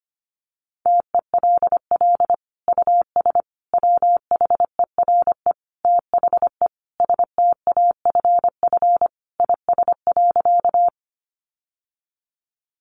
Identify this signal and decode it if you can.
Morse code, heard as a keyed tone, about 25 wpm, 700 Hz. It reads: TELL US WHERE THE STAFF IS.
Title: Sentences from Top 1200 Words 25wpm